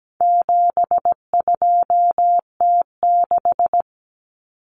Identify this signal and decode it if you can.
Morse code, heard as a keyed tone, about 17 wpm, 700 Hz. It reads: 72T6
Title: Common Cut Numbers - 4-Digit 17wpm